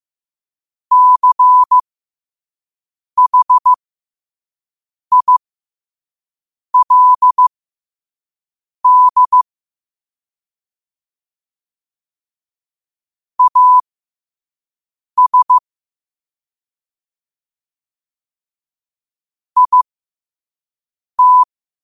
Morse code, heard as a keyed tone, about 15 wpm, spaced as for 5 wpm, 1000 Hz. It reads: CHILD AS IT
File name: qotdfiles / 5wpm